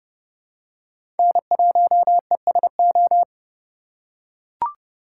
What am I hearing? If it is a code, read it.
Morse code, heard as a keyed tone, about 30 wpm, 700 Hz. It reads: N1ESO
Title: USA Call Signs - 5 Characters - 4x Character Spacing - 30wpm